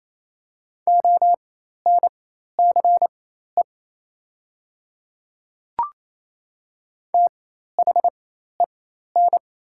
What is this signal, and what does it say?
Morse code, heard as a keyed tone, about 27 wpm, spaced as for 13 wpm, 700 Hz. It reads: ONCE THEN
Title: 4x Character Spacing - Top 300 Words - 28wpm